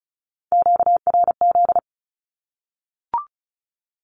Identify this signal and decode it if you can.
Morse code, heard as a keyed tone, about 34 wpm, 700 Hz. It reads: QRZ
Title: Q-Codes 35wpm